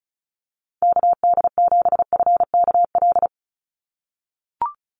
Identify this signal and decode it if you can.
Morse code, heard as a keyed tone, about 34 wpm, 700 Hz. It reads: KD7FKL
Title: USA Call Signs 35wpm